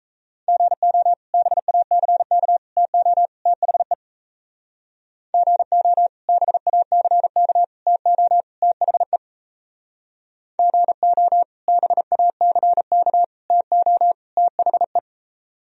Morse code, heard as a keyed tone, about 38 wpm, 700 Hz. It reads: GO BACK TO THE GO BACK TO THE GO BACK TO THE